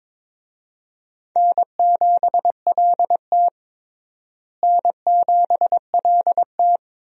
Morse code, heard as a keyed tone, about 22 wpm, 700 Hz. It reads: N7LT N7LT
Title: USA Call Signs - 4 Characters - Repeated 2x - 22wpm